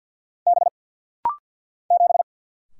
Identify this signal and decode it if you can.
Morse code, heard as a keyed tone, about 50 wpm, 700 Hz. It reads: B 7
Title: Single Letter-Number Alt - Rapid-Fire 50wpm